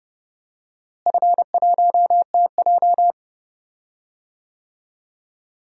Morse code, heard as a keyed tone, about 30 wpm, 700 Hz. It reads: F1TJ
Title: DXCC-100 Call Signs - 4 Characters - Repeated 2x - 30wpm